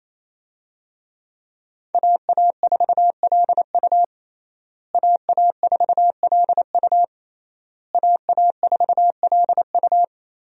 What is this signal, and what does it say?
Morse code, heard as a keyed tone, about 28 wpm, 700 Hz. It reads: AA4LU AA4LU AA4LU
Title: USA Call Signs - 5 Characters - Repeated 3x - 28wpm